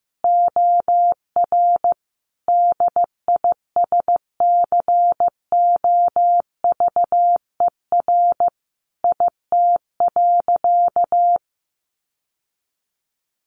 Morse code, heard as a keyed tone, about 15 wpm, 700 Hz. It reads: OR DISCOVER IT.
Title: Sentences from Top 1100 Words 15wpm